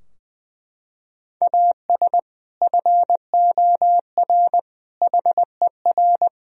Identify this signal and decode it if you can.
Morse code, heard as a keyed tone, about 20 wpm, 700 Hz. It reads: AS FOR HER
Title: Sentences from Top 1000 Words - Review Part 2 - 20wpm